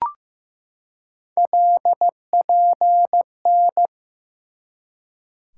Morse code, heard as a keyed tone, about 15 wpm, 700 Hz. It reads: LPN
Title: K1USN SST QSO Practice 15wpm